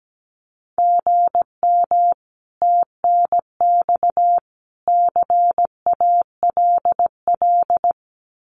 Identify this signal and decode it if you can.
Morse code, heard as a keyed tone, about 17 wpm, 700 Hz. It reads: GM TNX CALL
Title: Common QSO Phrases 17wpm